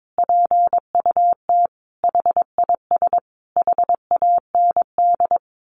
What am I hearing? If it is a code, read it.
Morse code, heard as a keyed tone, about 22 wpm, 700 Hz. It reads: PUT HIS HAND
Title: A Sound Of Thunder 22wpm